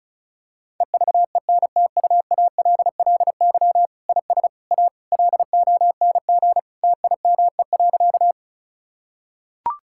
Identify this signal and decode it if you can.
Morse code, heard as a keyed tone, about 36 wpm, 700 Hz. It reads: EVENTUALLY IS A LONG TIME.
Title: Sentences from Top 1000 Words - Review Part 6 - 35wpm